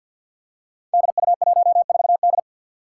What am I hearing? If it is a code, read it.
Morse code, heard as a keyed tone, about 53 wpm, 700 Hz. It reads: DU14D
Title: Five Challenging Letter-Number Combinations 50wpm